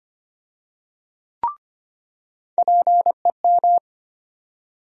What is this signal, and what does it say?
Morse code, heard as a keyed tone, about 25 wpm, 700 Hz. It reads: PEM
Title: USA Call Sign Postfixes - 3 Letters 25wpm